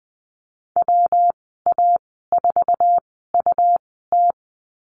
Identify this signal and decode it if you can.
Morse code, heard as a keyed tone, about 19 wpm, spaced as for 14 wpm, 700 Hz. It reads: WA4UT
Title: 147 Lesson 24g - Letter y - Call Signs 20wpm